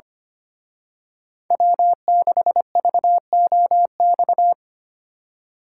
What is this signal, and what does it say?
Morse code, heard as a keyed tone, about 29 wpm, 700 Hz. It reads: W6VOX
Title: Speed Racing - USA 1x3 Call Signs 25wpm